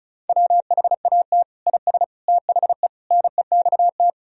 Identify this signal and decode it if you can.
Morse code, heard as a keyed tone, about 36 wpm, 700 Hz. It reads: WHAT IS THE NEXT
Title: Sentences from Top 1000 Words - Review Part 2 - 35wpm